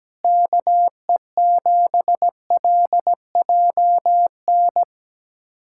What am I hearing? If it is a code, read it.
Morse code, heard as a keyed tone, about 17 wpm, 700 Hz. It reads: KE7LJN
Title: USA Callsign DE Callsign 17wpm